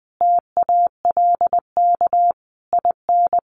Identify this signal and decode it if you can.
Morse code, heard as a keyed tone, about 20 wpm, 700 Hz. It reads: TALK IN
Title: Sentences from Top 700 Words 20wpm